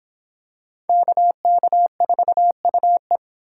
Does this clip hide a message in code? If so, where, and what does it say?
Yes: Morse code from 0.9 to 3.2 s, heard as a keyed tone, about 25 wpm, 700 Hz. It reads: KK4UE